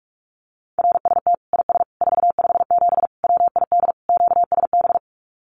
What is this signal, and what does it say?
Morse code, heard as a keyed tone, about 44 wpm, 700 Hz. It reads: RST IS 457 WID QSB
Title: Common QSO Phrases 45wpm